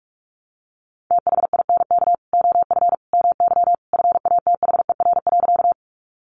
Morse code, heard as a keyed tone, about 44 wpm, 700 Hz. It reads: THINK OF MY FATHER.
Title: Sentences from Top 300 Words 45wpm